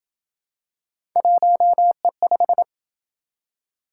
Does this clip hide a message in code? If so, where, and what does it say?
Yes: Morse code from 1.2 to 2.6 s, heard as a keyed tone, about 27 wpm, 700 Hz. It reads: 1E5